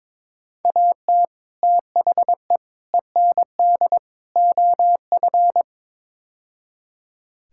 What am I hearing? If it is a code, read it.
Morse code, heard as a keyed tone, about 22 wpm, 700 Hz. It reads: AT THE END OF
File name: practice / 22wpm / Build A Phrase - 5-Word Phrases - 22wpm